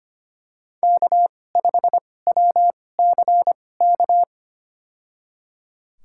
Morse code, heard as a keyed tone, about 25 wpm, spaced as for 18 wpm, 700 Hz. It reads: K5WCK
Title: USA Call Signs - 2x Character Spacing - 25wpm